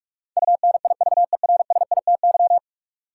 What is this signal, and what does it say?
Morse code, heard as a keyed tone, about 46 wpm, 700 Hz. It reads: UNIVERSITY
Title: Sentences from Top 1000 Words - Review Part 4 - New Word Drill - 45wpm